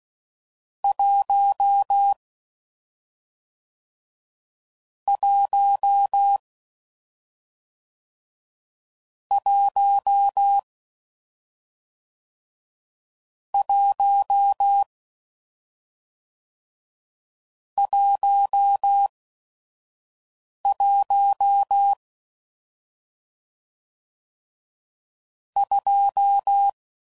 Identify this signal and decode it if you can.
Morse code, heard as a keyed tone, about 16 wpm, spaced as for 5 wpm, 800 Hz. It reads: 1 1 1 1 11 2